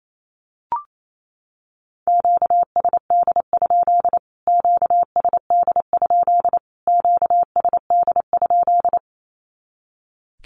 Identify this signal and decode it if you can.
Morse code, heard as a keyed tone, about 28 wpm, 700 Hz. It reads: QSD? QSD? QSD?
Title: Q-Codes Repeated 3x 28wpm